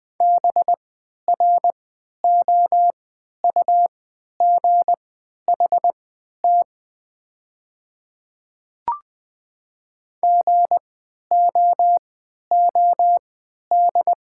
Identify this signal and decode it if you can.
Morse code, heard as a keyed tone, about 20 wpm, spaced as for 11 wpm, 690 Hz. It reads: BROUGHT GOOD
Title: 3x Character Spacing - Top 500 Words - 20wpm